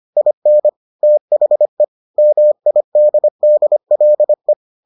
Morse code, heard as a keyed tone, about 25 wpm, 600 Hz. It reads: IN THE MIDDLE